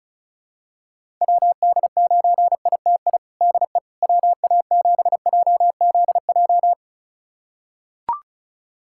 Morse code, heard as a keyed tone, about 34 wpm, 700 Hz. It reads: WD9ITI DE WA7JZJ
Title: USA Callsign DE Callsign - 6 Characters - 35wpm